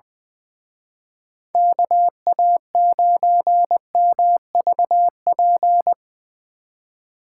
Morse code, heard as a keyed tone, about 22 wpm, 700 Hz. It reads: KA9MVP
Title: Speed Racing - USA 2x3 Call Signs 20wpm